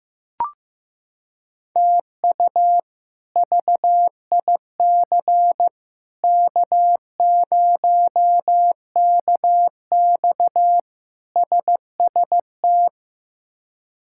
Morse code, heard as a keyed tone, about 15 wpm, 700 Hz. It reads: TU VIC K0KX SST
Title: K1USN SST QSO Practice - 15wpm